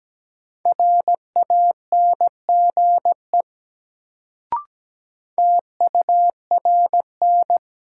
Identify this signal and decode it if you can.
Morse code, heard as a keyed tone, about 17 wpm, 700 Hz. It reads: RANGE TURN
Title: Sets of 4 Words - New Word Drill - 17wpm